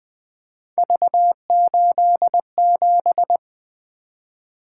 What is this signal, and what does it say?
Morse code, heard as a keyed tone, about 20 wpm, 700 Hz. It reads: V87
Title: Three Letter-Number Combinations - 20wpm